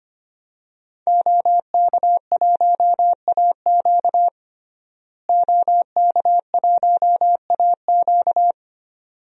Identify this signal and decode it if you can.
Morse code, heard as a keyed tone, about 25 wpm, 700 Hz. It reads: OK1AQ OK1AQ
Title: DXCC-100 Call Signs - 5 Characters - Repeated 2x - 25wpm